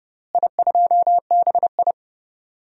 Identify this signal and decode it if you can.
Morse code, heard as a keyed tone, about 30 wpm, 700 Hz. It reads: I2BI
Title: DXCC-100 Call Signs - 4 Characters - Repeated 2x - 30wpm